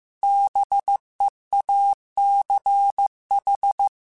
Morse code, heard as a keyed tone, about 15 wpm, 800 Hz. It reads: BEACH